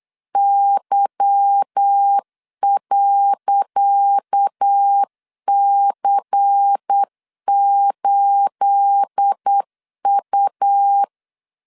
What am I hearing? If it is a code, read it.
Morse code, heard as a keyed tone, about 8 wpm, 800 Hz. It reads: Y.C8U
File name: LEZIONI / 10 SIGNS / 10C DOT 8wpm 3649